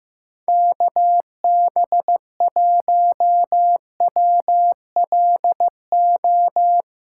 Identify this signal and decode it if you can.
Morse code, heard as a keyed tone, about 15 wpm, 700 Hz. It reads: KB1WLO
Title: USA Callsign DE Callsign 15wpm